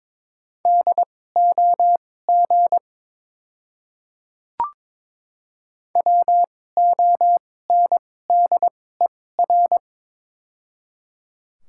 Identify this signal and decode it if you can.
Morse code, heard as a keyed tone, about 22 wpm, spaced as for 16 wpm, 700 Hz. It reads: DOG WONDER
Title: 2x Character Spacing - Top 500 Words - 22wpm